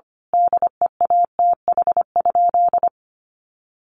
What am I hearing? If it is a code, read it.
Morse code, heard as a keyed tone, about 25 wpm, 700 Hz. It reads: DEATH?
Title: Sentences from Top 1100 Words 25wpm